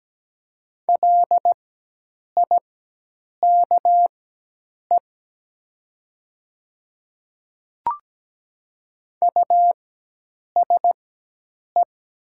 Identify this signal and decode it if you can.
Morse code, heard as a keyed tone, about 17 wpm, spaced as for 8 wpm, 700 Hz. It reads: LIKE USE
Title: 4x Character Spacing - Top 200 Words - 17wpm